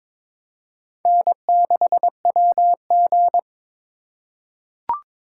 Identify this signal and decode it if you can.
Morse code, heard as a keyed tone, about 22 wpm, 700 Hz. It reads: N6WG
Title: K1USN SST QSO Practice - 22wpm